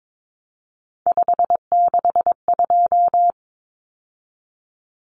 Morse code, heard as a keyed tone, about 22 wpm, 700 Hz. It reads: 562
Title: Numbers - 3-Digit 22wpm